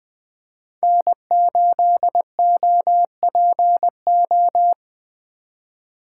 Morse code, heard as a keyed tone, about 19 wpm, 700 Hz. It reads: N8OPO